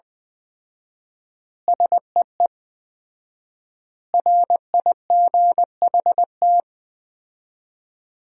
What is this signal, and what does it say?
Morse code, heard as a keyed tone, about 20 wpm, 700 Hz. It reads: SEE RIGHT
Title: 4x Word-Spacing - Sentences from Top 200 Words - 20wpm